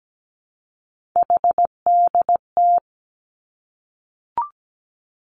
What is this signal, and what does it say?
Morse code, heard as a keyed tone, about 17 wpm, 700 Hz. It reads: HDT
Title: Three Challenging Letter-Number Combinations 17wpm